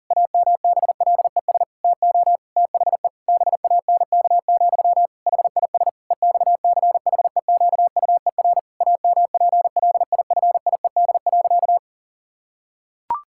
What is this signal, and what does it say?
Morse code, heard as a keyed tone, about 37 wpm, 700 Hz. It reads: AMBLES TO THE BANK, HIS EXCHEQUER AMPLIFIED.
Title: Pangram Challenge v2 - 40wpm